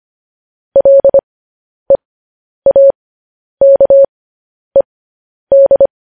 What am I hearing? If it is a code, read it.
Morse code, heard as a keyed tone, about 25 wpm, spaced as for 10 wpm, 550 Hz. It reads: LEAKED
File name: podcast / News headlines-1x-25x10